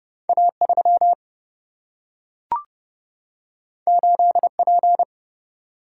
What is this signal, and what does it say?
Morse code, heard as a keyed tone, about 30 wpm, 700 Hz. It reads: A3 8P
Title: Single-Letter Single-Number 30wpm